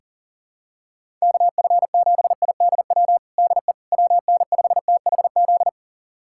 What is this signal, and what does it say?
Morse code, heard as a keyed tone, about 40 wpm, 690 Hz. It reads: KF7IDW DE WN5THZ
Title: USA Callsign DE Callsign - 6 Characters - 40wpm